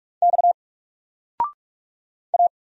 Morse code, heard as a keyed tone, about 46 wpm, 700 Hz. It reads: X A
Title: Single Letter-Number Alt 45wpm